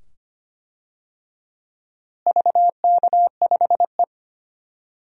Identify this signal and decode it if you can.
Morse code, heard as a keyed tone, about 25 wpm, 700 Hz. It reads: VK5E
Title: DXCC-100 Call Signs - 4 Characters - Repeated 2x - 25wpm